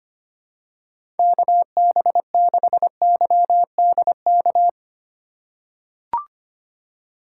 Morse code, heard as a keyed tone, about 25 wpm, 700 Hz. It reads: KB6YDK
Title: USA Call Signs - 6 Characters - 2x Character Spacing - 25wpm